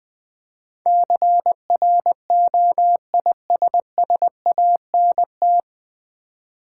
Morse code, heard as a keyed tone, about 20 wpm, 700 Hz. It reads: CROISSANT
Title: Easy Words with Hints - 20wpm